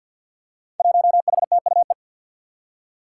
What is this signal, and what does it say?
Morse code, heard as a keyed tone, about 50 wpm, 700 Hz. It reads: 1HTUE
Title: Five Challenging Letter-Number Combinations 50wpm